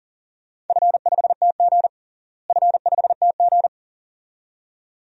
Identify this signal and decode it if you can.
Morse code, heard as a keyed tone, about 40 wpm, 700 Hz. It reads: F5TG F5TG
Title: DXCC-100 Call Signs - 4 Characters - Repeated 2x - 40wpm